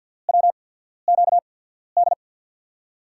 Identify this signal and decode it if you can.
Morse code, heard as a keyed tone, about 47 wpm, spaced as for 13 wpm, 700 Hz. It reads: WQD
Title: USA Call Signs - 6 Characters - 8x Character Spacing - 50wpm